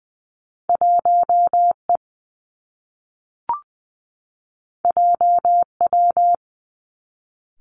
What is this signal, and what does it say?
Morse code, heard as a keyed tone, about 20 wpm, 700 Hz. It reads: 1 E J W
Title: Two Challenging Letter-Number Combinations 20wpm